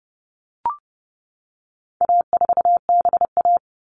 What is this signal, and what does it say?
Morse code, heard as a keyed tone, about 30 wpm, 700 Hz. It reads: A4BA
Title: All Cut Numbers - 4-Digit 30wpm